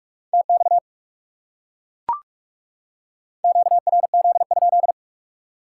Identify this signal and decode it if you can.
Morse code, heard as a keyed tone, about 39 wpm, 700 Hz. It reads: TX QRZ?